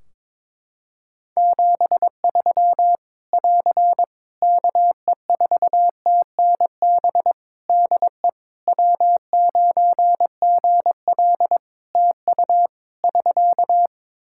Morse code, heard as a keyed tone, about 22 wpm, 700 Hz. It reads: 73 <AR> KE4TNB DE W9GL TU <SK>
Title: K1USN SST QSO Practice 22wpm